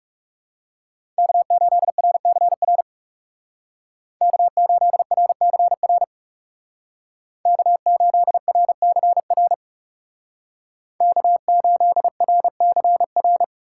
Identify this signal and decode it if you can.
Morse code, heard as a keyed tone, about 35 wpm, 700 Hz. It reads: K8RCR K8RCR K8RCR K8RCR
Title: Speed Racing - USA 1x3 Call Signs 30wpm